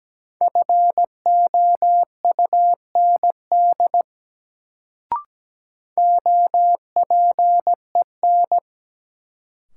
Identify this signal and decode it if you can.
Morse code, heard as a keyed tone, about 17 wpm, 700 Hz. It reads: FOUND OPEN
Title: Sentences from Top 300 Words - New Word Drill - 17wpm